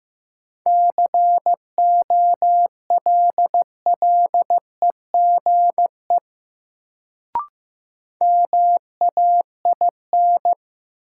Morse code, heard as a keyed tone, about 15 wpm, 700 Hz. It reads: COLLEGE MAIN
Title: Sentences from Top 1300 Words - New Word Drill Alt - 15wpm